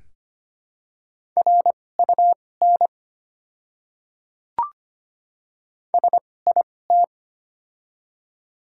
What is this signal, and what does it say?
Morse code, heard as a keyed tone, about 25 wpm, spaced as for 18 wpm, 700 Hz. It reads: RUN SIT
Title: 2x Character Spacing - Top 500 Words - 25wpm